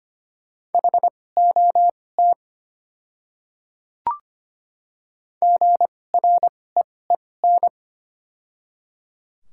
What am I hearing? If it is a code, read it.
Morse code, heard as a keyed tone, about 25 wpm, spaced as for 18 wpm, 700 Hz. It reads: HOT GREEN
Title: 2x Character Spacing - Top 500 Words - 25wpm